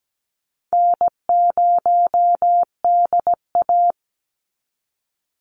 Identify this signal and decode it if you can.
Morse code, heard as a keyed tone, about 17 wpm, 700 Hz. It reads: N0DA
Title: NA SOTA Chaser Callsigns 17wpm